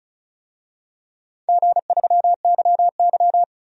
Morse code, heard as a keyed tone, about 36 wpm, 700 Hz. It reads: G3YY
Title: DXCC-100 Call Signs - 4 Characters - 35wpm